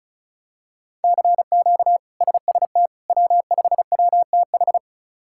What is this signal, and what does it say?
Morse code, heard as a keyed tone, about 36 wpm, 700 Hz. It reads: CQ SST W5WTH
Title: K1USN SST QSO Practice - 35wpm